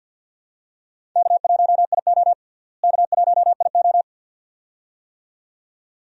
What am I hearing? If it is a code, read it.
Morse code, heard as a keyed tone, about 46 wpm, 700 Hz. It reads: K1IO K1IO
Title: USA Call Signs - 4 Characters - Repeated 2x - 50wpm